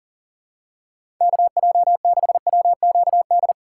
Morse code, heard as a keyed tone, about 37 wpm, 700 Hz. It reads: KJ6WQD
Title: USA Call Signs - 6 Characters - 2x Character Spacing - 40wpm